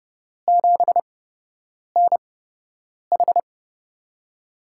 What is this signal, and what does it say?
Morse code, heard as a keyed tone, about 32 wpm, spaced as for 8 wpm, 700 Hz. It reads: 7NH
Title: USA Call Signs - 5 Characters - 8x Character Spacing - 30wpm